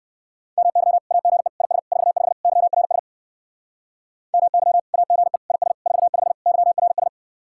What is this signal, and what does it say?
Morse code, heard as a keyed tone, about 60 wpm, 700 Hz. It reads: MY AGE IS 45 YRS MY AGE IS 45 YRS